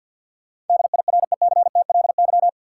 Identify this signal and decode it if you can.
Morse code, heard as a keyed tone, about 53 wpm, 700 Hz. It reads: DIRECTLY
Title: Sentences from Top 800 Words - New Word Drill - 50wpm